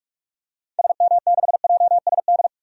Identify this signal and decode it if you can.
Morse code, heard as a keyed tone, about 46 wpm, 700 Hz. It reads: SM6JSD
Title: DXCC-100 Call Signs - 6 Characters - 45wpm